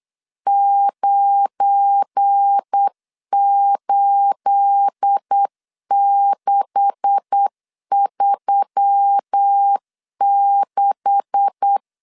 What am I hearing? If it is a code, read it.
Morse code, heard as a keyed tone, about 8 wpm, 800 Hz. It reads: 98636
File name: LEZIONI / 9 286 8wpm 3425